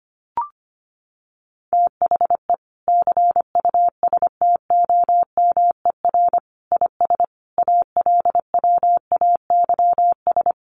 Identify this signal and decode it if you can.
Morse code, heard as a keyed tone, about 25 wpm, 700 Hz. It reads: THE CUSTOMER IS ALWAYS
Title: Sentences from Top 1100 Words 25wpm